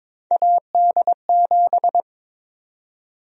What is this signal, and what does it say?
Morse code, heard as a keyed tone, about 22 wpm, 700 Hz. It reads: AD7
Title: USA Call Sign Prefixes - 22wpm